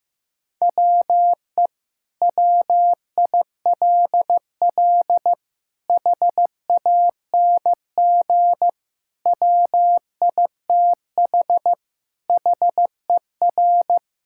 Morse code, heard as a keyed tone, about 15 wpm, 700 Hz. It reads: WE WILL HANG WITH HER
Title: Sentences from Top 1600 Words 15wpm